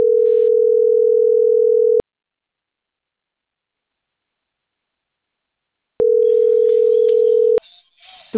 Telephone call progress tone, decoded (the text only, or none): ringback tone